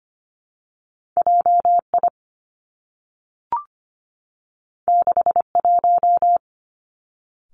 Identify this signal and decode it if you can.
Morse code, heard as a keyed tone, about 25 wpm, 700 Hz. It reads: JI 61
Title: Two Challenging Letter-Number Combinations 25wpm